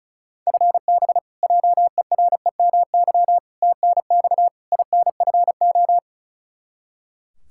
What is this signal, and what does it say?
Morse code, heard as a keyed tone, about 36 wpm, 700 Hz. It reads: FB JEREMY TNX INFO
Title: Practice QSO - Exchange 3 - More Info - 35wpm